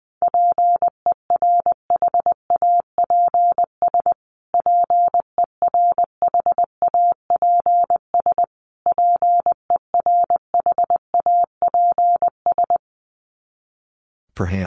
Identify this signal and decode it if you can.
Morse code, heard as a keyed tone, about 20 wpm, 700 Hz. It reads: PERHAPS PERHAPS PERHAPS
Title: Top 500 Words Repeated 20wpm